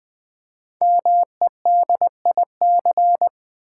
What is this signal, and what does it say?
Morse code, heard as a keyed tone, about 20 wpm, 700 Hz. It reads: MEDIC
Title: Build A Word - 5 Letters At A Time - 10-Letter Words - 20wpm